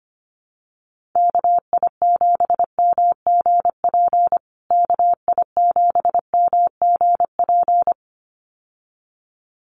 Morse code, heard as a keyed tone, about 25 wpm, 700 Hz. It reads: KI7MGP KI7MGP
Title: USA Call Signs Repeated 2x 25wpm